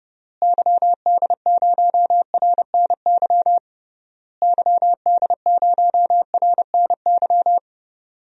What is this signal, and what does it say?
Morse code, heard as a keyed tone, about 30 wpm, 700 Hz. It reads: YD0RNY YD0RNY
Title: DXCC-100 Call Signs - 6 Characters - Repeated 3x - 30wpm